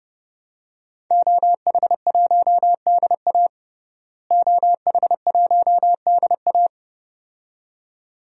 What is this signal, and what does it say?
Morse code, heard as a keyed tone, about 32 wpm, 700 Hz. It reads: OH1DA OH1DA
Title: DXCC-100 Call Signs - 5 Characters - Repeated 2x - 30wpm